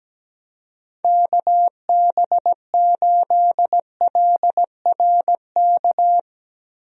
Morse code, heard as a keyed tone, about 17 wpm, 700 Hz. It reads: KB8LRK
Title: USA Call Signs - 6 Characters - Repeated 3x - 17wpm